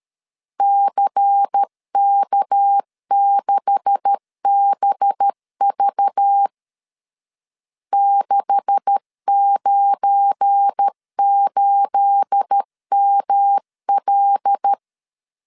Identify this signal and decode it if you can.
Morse code, heard as a keyed tone, about 13 wpm, 800 Hz. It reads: CK6BV 698ML
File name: LEZIONI / 9 286 12wpm 2305